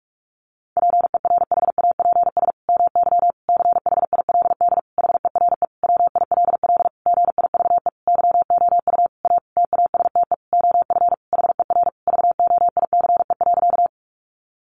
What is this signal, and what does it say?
Morse code, heard as a keyed tone, about 44 wpm, 700 Hz. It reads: PERHAPS MY CHILD HERE WILL GIVE YOU A TASTE OF HER VOICE.
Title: Sentences from Top 1100 Words 45wpm